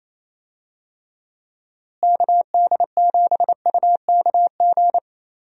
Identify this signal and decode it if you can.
Morse code, heard as a keyed tone, about 28 wpm, 700 Hz. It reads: KD7UKG
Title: USA Call Signs - 6 Characters - Repeated 2x - 28wpm